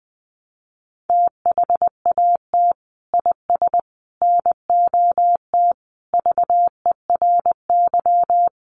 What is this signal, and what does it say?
Morse code, heard as a keyed tone, about 20 wpm, 700 Hz. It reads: THAT IS NOT VERY
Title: Sentences from Top 600 Words 20wpm